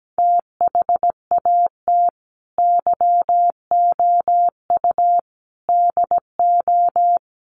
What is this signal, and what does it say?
Morse code, heard as a keyed tone, about 17 wpm, 700 Hz. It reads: THAT YOU DO
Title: Sets of 5 Words Encore 17wpm